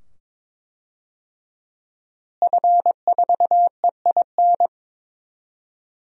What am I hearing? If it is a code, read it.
Morse code, heard as a keyed tone, about 22 wpm, 700 Hz. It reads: F4EIN
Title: DXCC-100 Call Signs - 5 Characters - Repeated 2x - 22wpm